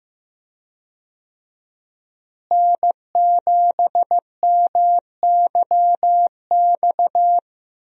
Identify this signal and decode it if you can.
Morse code, heard as a keyed tone, about 15 wpm, 700 Hz. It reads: N7MYX
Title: USA Call Signs - 5 Characters - Repeated 2x - 15wpm